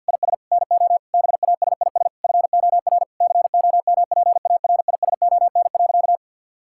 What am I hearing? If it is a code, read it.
Morse code, heard as a keyed tone, about 50 wpm, 700 Hz. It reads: IS NO BASIS FOR COMPARISON.
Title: Sentences from Top 1600 Words 50wpm